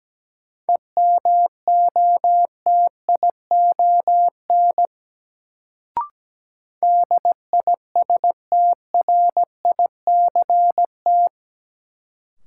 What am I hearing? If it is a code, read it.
Morse code, heard as a keyed tone, about 17 wpm, 700 Hz. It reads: EMOTION DISTRICT
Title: Sentences from Top 1600 Words - New Word Drill - 17wpm